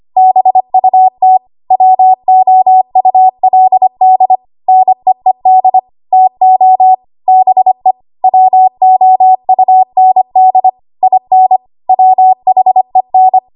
Morse code, heard as a keyed tone, about 25 wpm, 750 Hz. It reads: BUT WOULD NEED TO BE WOUND IN WHEN
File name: Archive / 25 WPM / 25WPM